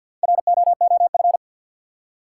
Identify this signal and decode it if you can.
Morse code, heard as a keyed tone, about 47 wpm, 700 Hz. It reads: ROOF
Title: Four Letter Words - 50wpm